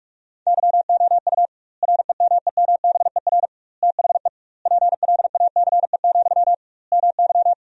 Morse code, heard as a keyed tone, about 46 wpm, 700 Hz. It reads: YOU REMEMBER THE PLACE, MY
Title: 2x Word-Spacing - Sentences from Top 400 Words - 45wpm